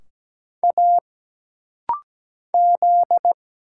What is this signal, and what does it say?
Morse code, heard as a keyed tone, about 17 wpm, 700 Hz. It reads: A Z
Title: Single Letters - Rapid-Fire 17wpm